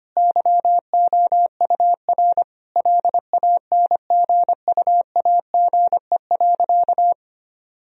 Morse code, heard as a keyed tone, about 25 wpm, 700 Hz. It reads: YOUR LANGUAGE.
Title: Sentences from Top 1000 Words - Review Part 3 - 25wpm